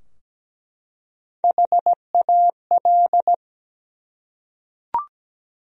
Morse code, heard as a keyed tone, about 17 wpm, 700 Hz. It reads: HAL
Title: USA Common Names - Max 5 Letters - 17wpm